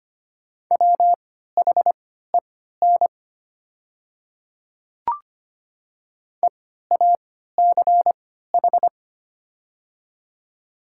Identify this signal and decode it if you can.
Morse code, heard as a keyed tone, about 25 wpm, spaced as for 14 wpm, 700 Hz. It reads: WHEN EACH
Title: 3x Character Spacing - Top 200 Words - 25wpm